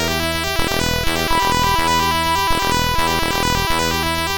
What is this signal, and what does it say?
There is no code.